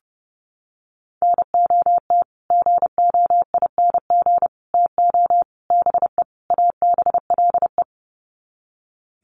Morse code, heard as a keyed tone, about 30 wpm, 700 Hz. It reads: NOT GOING TO BE ABLE
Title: Sets of 5 Words 30wpm